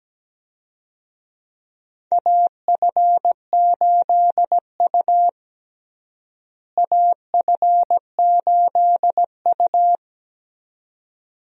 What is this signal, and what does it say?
Morse code, heard as a keyed tone, about 17 wpm, 700 Hz. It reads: AF8U AF8U
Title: USA Call Signs - 4 Characters - Repeated 2x - 17wpm